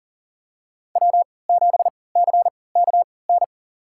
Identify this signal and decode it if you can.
Morse code, heard as a keyed tone, about 40 wpm, spaced as for 23 wpm, 700 Hz. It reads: W7CKN
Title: USA Call Signs - 3x Character Spacing - 40wpm